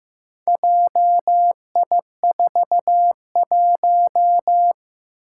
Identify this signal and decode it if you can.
Morse code, heard as a keyed tone, about 15 wpm, 700 Hz. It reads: JI41
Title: Four Challenging Letter-Number Combinations 15wpm